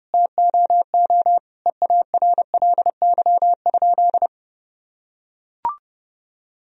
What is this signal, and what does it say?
Morse code, heard as a keyed tone, about 28 wpm, 700 Hz. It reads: TOO EARLY?
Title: Sentences from Top 1000 Words - Review Part 2 - 30wpm